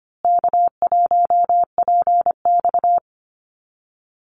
Morse code, heard as a keyed tone, about 25 wpm, 700 Hz. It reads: K1PX
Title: Easy Call Signs Repeated 3x - 25wpm